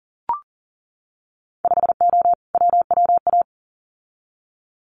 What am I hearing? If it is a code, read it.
Morse code, heard as a keyed tone, about 40 wpm, 700 Hz. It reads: 5O WWA
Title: Winter Field Day Exchanges Encore 40wpm